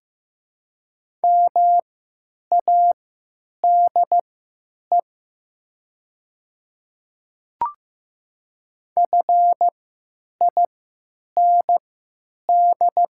Morse code, heard as a keyed tone, about 15 wpm, spaced as for 9 wpm, 700 Hz. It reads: MADE FIND
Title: 3x Character Spacing - Top 200 Words - 15wpm